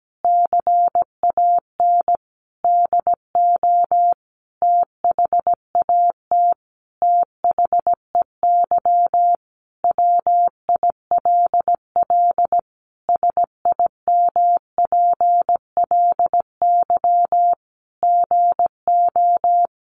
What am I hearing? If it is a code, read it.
Morse code, heard as a keyed tone, about 17 wpm, 700 Hz. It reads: CAN DO THAT THEY WILL SIMPLY GO
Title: Sentences from Top 600 Words 17wpm